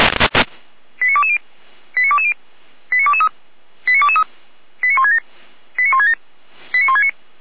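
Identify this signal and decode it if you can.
Five-tone selective call (ZVEI), heard as quick runs of five tones, tones of about 70 ms. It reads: 89220 89220 89223 89223 89167 89167 89169
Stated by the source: ZVEI five-tone selective call received on SDR.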